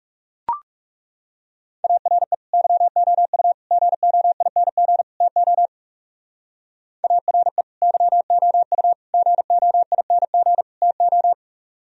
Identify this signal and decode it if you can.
Morse code, heard as a keyed tone, about 36 wpm, 700 Hz. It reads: ARE YOU GOING TO ARE YOU GOING TO
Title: Speed-Racing - Sets of 4 Words 30wpm